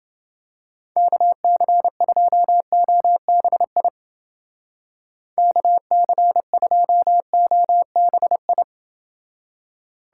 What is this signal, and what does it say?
Morse code, heard as a keyed tone, about 28 wpm, 700 Hz. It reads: KC2OBI KC2OBI